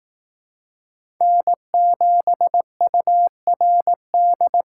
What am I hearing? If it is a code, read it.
Morse code, heard as a keyed tone, about 18 wpm, 700 Hz. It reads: N7URD
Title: Speed Racing - USA 1x3 Call Signs 15wpm